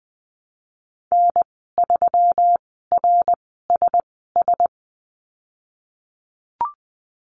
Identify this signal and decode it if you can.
Morse code, heard as a keyed tone, about 20 wpm, spaced as for 15 wpm, 700 Hz. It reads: N3RSS
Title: 154 Lesson 25g - Letter p - Call Signs 20wpm